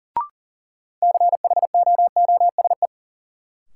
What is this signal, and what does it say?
Morse code, heard as a keyed tone, about 40 wpm, 700 Hz. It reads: CHOOSE
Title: Sentences from Top 700 Words - New Word Drill - 40wpm